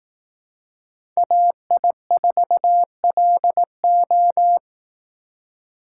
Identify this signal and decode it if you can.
Morse code, heard as a keyed tone, about 18 wpm, 700 Hz. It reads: AI4LO
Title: Speed Racing - USA Call Signs 15wpm